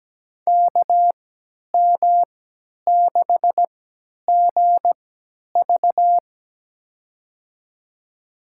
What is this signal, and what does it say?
Morse code, heard as a keyed tone, about 17 wpm, spaced as for 10 wpm, 700 Hz. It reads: KM6GV
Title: USA Call Signs - 5 Characters - 3x Character Spacing - 17wpm